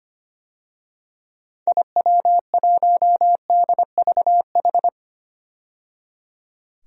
Morse code, heard as a keyed tone, about 25 wpm, 700 Hz. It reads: IW1DVH
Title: DXCC-100 Call Signs - 6 Characters - 25wpm